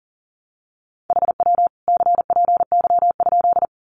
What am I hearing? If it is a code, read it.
Morse code, heard as a keyed tone, about 40 wpm, 700 Hz. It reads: HW CPY?
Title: Common QSO Phrases 40wpm